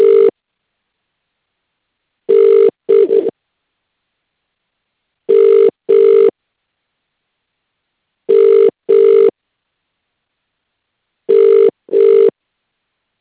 Ringback tone (UK), 400 + 450 Hz.